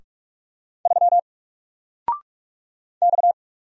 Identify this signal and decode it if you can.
Morse code, heard as a keyed tone, about 46 wpm, 700 Hz. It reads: 3 X
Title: Single Letter-Number Alt 45wpm